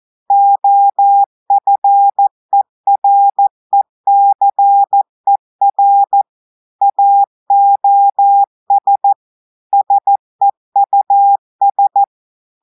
Morse code, heard as a keyed tone, about 14 wpm, 800 Hz. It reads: OFERECER AOS SEUS
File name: audios / portugais / portugais texte en clair 2 vitesse 14 mots minutes